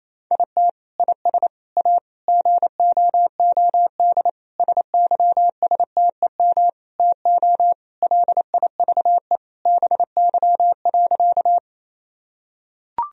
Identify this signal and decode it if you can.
Morse code, heard as a keyed tone, about 28 wpm, 700 Hz. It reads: IT IS A GOOD SYSTEM TO LIVE BY.
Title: Sentences from Top 1000 Words - Review Part 3 - 28wpm